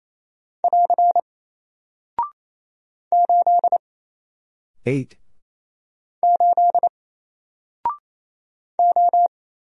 Morse code, heard as a keyed tone, about 28 wpm, 700 Hz. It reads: <AR> 8 8 O
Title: Single Character (Letter:Number:Punctuation:Prosign) Alt 28wpm